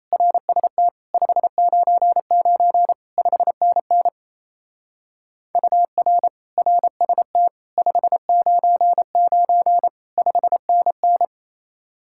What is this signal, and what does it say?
Morse code, heard as a keyed tone, about 34 wpm, 700 Hz. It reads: RST 599 5NN UR RST 599 5NN
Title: Speed-Racing Common QSO Phrases - 28wpm